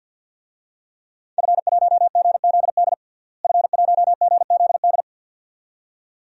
Morse code, heard as a keyed tone, about 52 wpm, 700 Hz. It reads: F1GZD F1GZD
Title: DXCC-100 Call Signs - 5 Characters - Repeated 2x - 50wpm